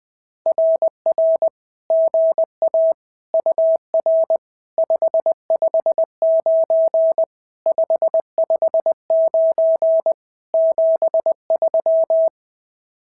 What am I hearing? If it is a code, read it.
Morse code, heard as a keyed tone, about 19 wpm, 650 Hz. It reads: RR GA UR 559 559 73
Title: SOTA QSO Practice 20wpm